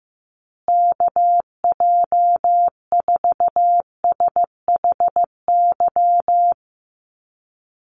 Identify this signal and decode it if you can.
Morse code, heard as a keyed tone, about 15 wpm, 700 Hz. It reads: KJ4SHY
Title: USA Call Signs Repeated 3x 15wpm